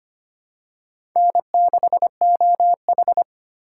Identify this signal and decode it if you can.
Morse code, heard as a keyed tone, about 25 wpm, 700 Hz. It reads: N6OH
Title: USA Call Signs - 4 Characters - 3x Character Spacing - 25wpm